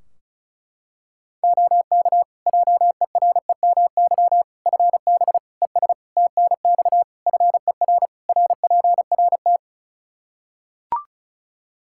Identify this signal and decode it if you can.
Morse code, heard as a keyed tone, about 36 wpm, 700 Hz. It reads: OK JEREMY FB ES TNX FER RPRT
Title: Practice QSO - Exchange 2 - Info - 35wpm